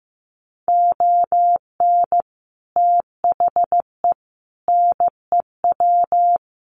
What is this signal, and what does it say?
Morse code, heard as a keyed tone, about 15 wpm, 700 Hz. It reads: ON THE NEW
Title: Sets of 5 Words Encore 15wpm